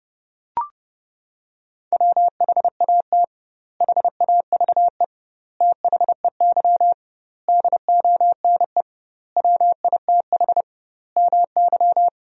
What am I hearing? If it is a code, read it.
Morse code, heard as a keyed tone, about 32 wpm, 700 Hz. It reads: WHAT HAVE THEY DONE WITH MY
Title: 2x Word-Spacing - Sentences from Top 500 Words - 30wpm